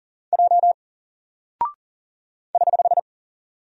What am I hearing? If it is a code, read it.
Morse code, heard as a keyed tone, about 40 wpm, 700 Hz. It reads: J <HH>
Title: Single Character (Letter:Number:Punctuation:Prosign) 40wpm